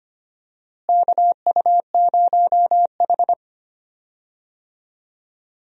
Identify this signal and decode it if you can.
Morse code, heard as a keyed tone, about 25 wpm, 700 Hz. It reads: KU0H
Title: USA Call Signs - 4 Characters - Repeated 3x - 25wpm